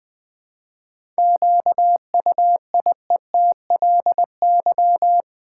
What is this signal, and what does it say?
Morse code, heard as a keyed tone, about 20 wpm, 700 Hz. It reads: QUIETLY
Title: 7 Letter Words - First 3 Letters Given - 20wpm